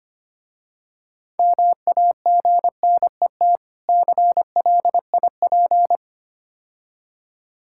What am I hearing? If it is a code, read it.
Morse code, heard as a keyed tone, about 28 wpm, 690 Hz. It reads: MAGNET CLIP
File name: practice / 25wpm / Difficult Words - Speed-Racing with Hints - 25wpm